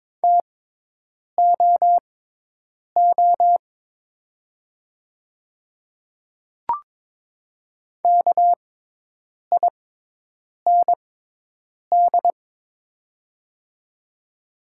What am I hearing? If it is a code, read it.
Morse code, heard as a keyed tone, about 22 wpm, spaced as for 8 wpm, 700 Hz. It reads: TOO KIND
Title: 6x Character Spacing - Top 300 Words - 22wpm